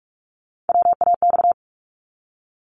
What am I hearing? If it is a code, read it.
Morse code, heard as a keyed tone, about 46 wpm, 700 Hz. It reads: WAX
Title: Multi-letter Phonemes 45wpm